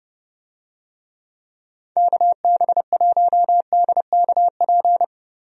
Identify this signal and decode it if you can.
Morse code, heard as a keyed tone, about 30 wpm, 700 Hz. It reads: KB1DKP
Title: USA Call Signs - 6 Characters - 30wpm